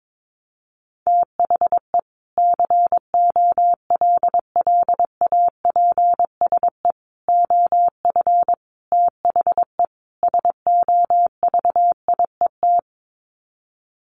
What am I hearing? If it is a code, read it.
Morse code, heard as a keyed tone, about 22 wpm, 700 Hz. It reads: THE COLLAPSE OF THE SOVIET
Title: Sets of 5 Words Encore 22wpm